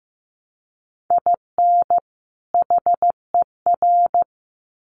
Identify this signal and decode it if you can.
Morse code, heard as a keyed tone, about 15 wpm, 700 Hz. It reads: IN HER
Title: Sets of 2 Words 15wpm